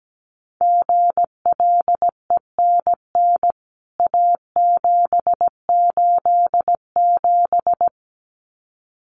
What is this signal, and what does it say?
Morse code, heard as a keyed tone, about 17 wpm, 700 Hz. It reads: GLENN A787
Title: CWT Exchange 17wpm